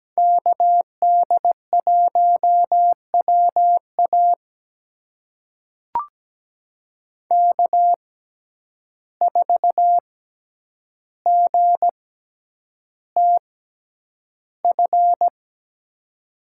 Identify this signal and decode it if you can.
Morse code, heard as a keyed tone, about 17 wpm, spaced as for 9 wpm, 700 Hz. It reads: KD1WA K4GTF
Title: USA Call Signs - 5 Characters - 6x Character Spacing - 17wpm